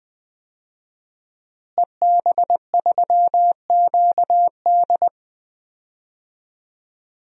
Morse code, heard as a keyed tone, about 20 wpm, 700 Hz. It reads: EB3QD
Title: DXCC-100 Call Signs - 5 Characters - Repeated 2x - 20wpm